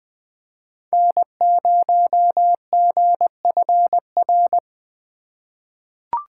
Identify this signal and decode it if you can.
Morse code, heard as a keyed tone, about 20 wpm, 700 Hz. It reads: N0GFR
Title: USA Call Signs - 5 Characters - 2x Character Spacing - 20wpm